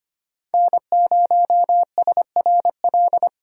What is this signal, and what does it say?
Morse code, heard as a keyed tone, about 25 wpm, 700 Hz. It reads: N0SRL
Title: USA Call Signs - 5 Characters - 3x Character Spacing - 25wpm